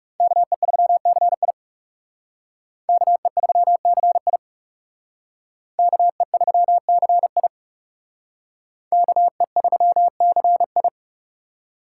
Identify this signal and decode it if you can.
Morse code, heard as a keyed tone, about 37 wpm, 700 Hz. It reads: KE3CI KE3CI KE3CI KE3CI
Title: Speed Racing - USA 2x2 Call Signs 30wpm